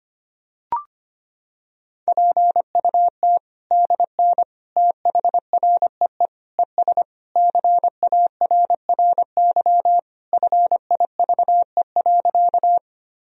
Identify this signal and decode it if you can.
Morse code, heard as a keyed tone, about 25 wpm, 700 Hz. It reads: PUT DN THREE ES CARRY FIVE.
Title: Sentences from Top 400 Words and Abbreviations 25wpm